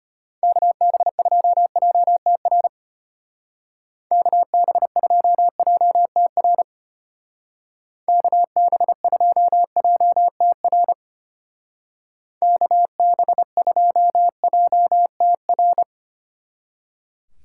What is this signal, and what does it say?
Morse code, heard as a keyed tone, about 31 wpm, 700 Hz. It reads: KB2JTR KB2JTR KB2JTR KB2JTR